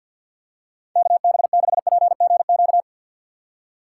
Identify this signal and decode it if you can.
Morse code, heard as a keyed tone, about 52 wpm, 700 Hz. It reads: KB6PGQ
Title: USA Call Signs - 6 Characters - 6x Character Spacing - 50wpm